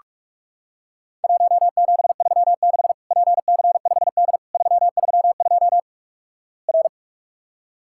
Morse code, heard as a keyed tone, about 48 wpm, 700 Hz. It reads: 1736 PC5D 332 R
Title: Worked All Europe DX Contest - QTC Exchange - 45wpm